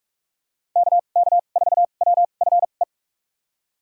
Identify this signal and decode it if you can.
Morse code, heard as a keyed tone, about 41 wpm, spaced as for 31 wpm, 700 Hz. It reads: KK4WFE
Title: USA Call Signs - 6 Characters - 2x Character Spacing - 45wpm